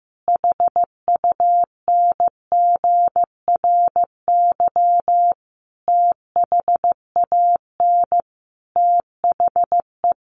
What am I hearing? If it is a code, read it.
Morse code, heard as a keyed tone, about 15 wpm, 700 Hz. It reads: HUNGRY THAN THE
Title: Sentences from Top 1400 Words 15wpm